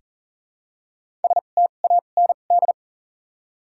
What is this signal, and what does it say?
Morse code, heard as a keyed tone, about 40 wpm, spaced as for 29 wpm, 700 Hz. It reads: STAND
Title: 2x Character Spacing - Top 300 Words - 40wpm